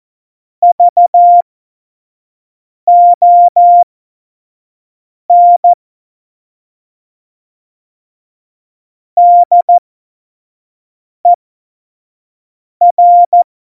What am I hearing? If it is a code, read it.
Morse code, heard as a keyed tone, about 14 wpm, spaced as for 5 wpm, 700 Hz. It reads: VON DER
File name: Morsekurs / Witze / Witz 05. Januar 2021, 25 F70